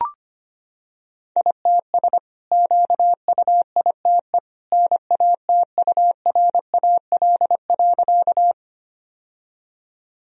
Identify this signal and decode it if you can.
Morse code, heard as a keyed tone, about 25 wpm, 690 Hz. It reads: ITS QUITE NATURAL.
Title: Sentences from Top 1000 Words - Review Part 4 - 25wpm